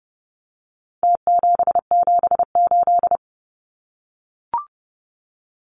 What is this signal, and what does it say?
Morse code, heard as a keyed tone, about 30 wpm, 700 Hz. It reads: T778
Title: Numbers - Contest Style - 4-Digit 30wpm